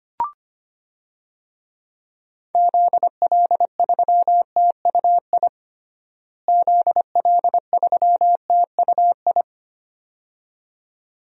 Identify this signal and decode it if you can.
Morse code, heard as a keyed tone, about 25 wpm, 700 Hz. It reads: ZL3TUI ZL3TUI
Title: DXCC-100 Call Signs - 6 Characters - Repeated 2x - 25wpm